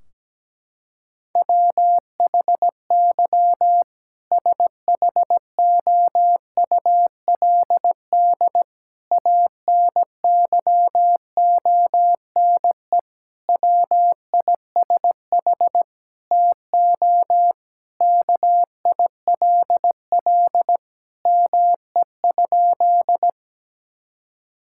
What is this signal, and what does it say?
Morse code, heard as a keyed tone, about 17 wpm, 700 Hz. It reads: WHY SHOULD ANYONE WISH TO KILL ME?
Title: Sentences from Top 1600 Words 17wpm